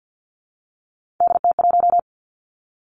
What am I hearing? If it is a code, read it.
Morse code, heard as a keyed tone, about 50 wpm, 700 Hz. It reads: DT1
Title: Three Challenging Letter-Number Combinations 50wpm